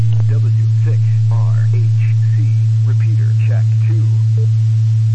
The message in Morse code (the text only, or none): none